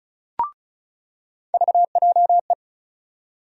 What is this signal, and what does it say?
Morse code, heard as a keyed tone, about 34 wpm, 700 Hz. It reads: VJE
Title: USA Call Sign Postfixes - 3 Letters 35wpm